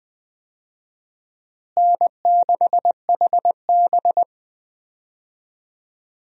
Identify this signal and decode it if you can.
Morse code, heard as a keyed tone, about 20 wpm, 700 Hz. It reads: N6HB
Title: USA Call Signs - 4 Characters - Repeated 2x - 20wpm